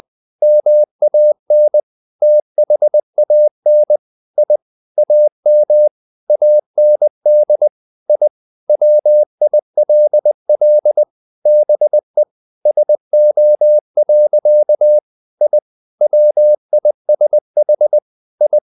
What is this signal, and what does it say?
Morse code, heard as a keyed tone, about 20 wpm, 600 Hz. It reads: MAN THAN I AM AND I WILL BE SO. I WISH I
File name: the old man and the sea / the old man and the sea part 1 20wpm